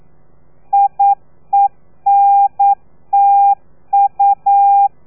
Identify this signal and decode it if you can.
Morse code, heard as a keyed tone, about 9 wpm, 800 Hz. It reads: IENTU
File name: telegrafia / i2tfj / Lez3 40